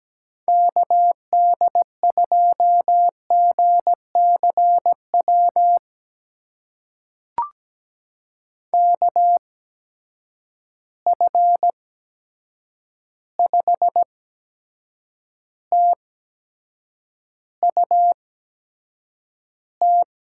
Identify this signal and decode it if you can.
Morse code, heard as a keyed tone, about 17 wpm, spaced as for 7 wpm, 700 Hz. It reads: KD2GCW KF5TUT
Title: USA Call Signs - 6 Characters - 8x Character Spacing - 17wpm